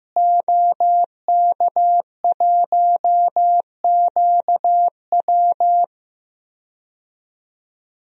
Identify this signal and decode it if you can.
Morse code, heard as a keyed tone, about 15 wpm, 700 Hz. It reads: OK1QW
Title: DXCC-100 Call Signs - 5 Characters - Repeated 2x - 15wpm